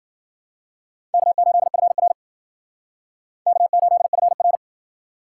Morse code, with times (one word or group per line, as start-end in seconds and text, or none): 1.1-2.1 K8FR
3.5-4.6 K8FR